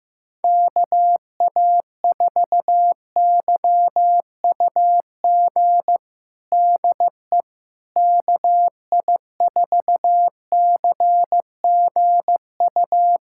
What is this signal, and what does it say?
Morse code, heard as a keyed tone, about 15 wpm, 700 Hz. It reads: KA4YUG DE KI4CGU
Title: USA Callsign DE Callsign - 6 Characters - 15wpm